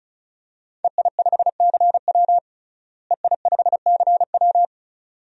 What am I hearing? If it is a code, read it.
Morse code, heard as a keyed tone, about 36 wpm, 700 Hz. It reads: EI5CW EI5CW
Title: DXCC-100 Call Signs - 5 Characters - Repeated 2x - 35wpm